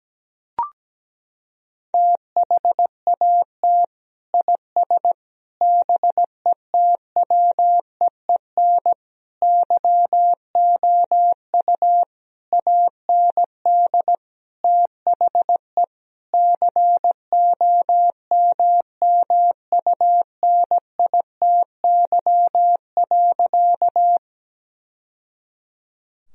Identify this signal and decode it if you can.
Morse code, heard as a keyed tone, about 17 wpm, 700 Hz. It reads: THAT IS BETWEEN YOU AND THE COMMUNITY.
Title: Sentences from Top 1000 Words - Review Part 4 - 17wpm